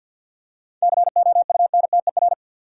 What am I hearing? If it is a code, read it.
Morse code, heard as a keyed tone, about 50 wpm, 700 Hz. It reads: COUNTER
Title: Seven Letter Words - 50wpm